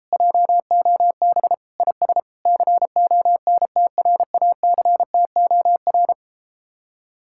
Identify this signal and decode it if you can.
Morse code, heard as a keyed tone, about 33 wpm, 700 Hz. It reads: JOB IS CONTRACTOR